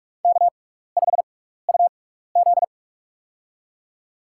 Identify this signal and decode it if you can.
Morse code, heard as a keyed tone, about 46 wpm, spaced as for 16 wpm, 700 Hz. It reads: K5UZ
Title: USA Call Signs - 4 Characters - 6x Character Spacing - 45wpm